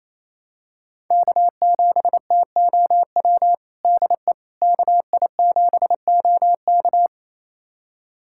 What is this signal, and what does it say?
Morse code, heard as a keyed tone, about 27 wpm, 700 Hz. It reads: K7TOW DE KI7OK
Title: USA Callsign DE Callsign - 5 Characters - 28wpm